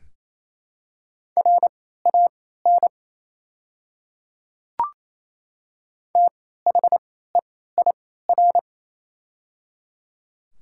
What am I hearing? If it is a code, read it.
Morse code, heard as a keyed tone, about 28 wpm, spaced as for 16 wpm, 700 Hz. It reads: RAN THEIR
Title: 3x Character Spacing - Top 500 Words - 28wpm